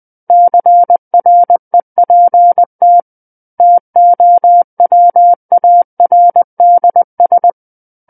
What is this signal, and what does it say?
Morse code, heard as a keyed tone, about 20 wpm, 700 Hz. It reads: CREPT TOWARDS
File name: 20 WPM